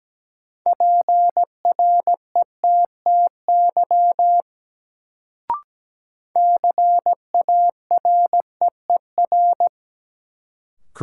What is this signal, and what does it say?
Morse code, heard as a keyed tone, about 17 wpm, 700 Hz. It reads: PRETTY CAREER
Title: Sentences from Top 1000 Words - New Word Drill - 17wpm